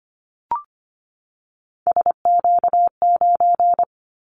Morse code, heard as a keyed tone, about 25 wpm, 700 Hz. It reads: SQ9
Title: Callsign Prefixes for Top 25 DXCC - 25wpm